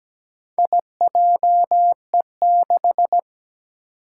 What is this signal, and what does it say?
Morse code, heard as a keyed tone, about 17 wpm, 700 Hz. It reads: IJE6
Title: Four Challenging Letter-Number Combinations 17wpm